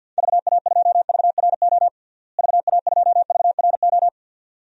Morse code, heard as a keyed tone, about 50 wpm, 700 Hz. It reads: VA2VRO VA2VRO
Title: DXCC-100 Call Signs - 6 Characters - Repeated 2x - 50wpm